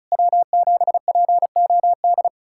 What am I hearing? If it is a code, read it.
Morse code, heard as a keyed tone, about 36 wpm, 690 Hz. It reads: W7POD
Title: SOTA QSO Practice 35wpm